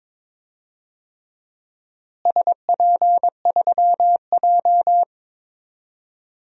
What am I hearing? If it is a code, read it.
Morse code, heard as a keyed tone, about 22 wpm, 700 Hz. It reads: SP3J
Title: DXCC-100 Call Signs - 4 Characters - Repeated 3x - 22wpm